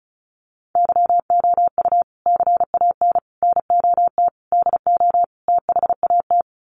Morse code, heard as a keyed tone, about 34 wpm, 700 Hz. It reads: YOU CAN NOT DO THAT
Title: Sets of 5 Words 35wpm